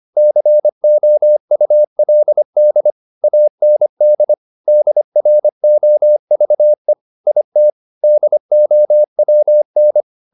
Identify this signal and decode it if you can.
Morse code, heard as a keyed tone, about 25 wpm, 600 Hz. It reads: COULD AND DROVE IT DOWN